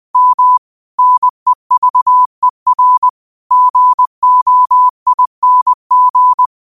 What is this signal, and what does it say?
Morse code, heard as a keyed tone, about 20 wpm, 1000 Hz. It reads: M NEVER GOING